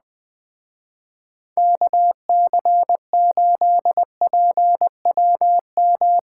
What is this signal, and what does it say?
Morse code, heard as a keyed tone, about 19 wpm, 700 Hz. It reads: KC8PWM
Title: Speed Racing - USA 2x3 Call Signs 17wpm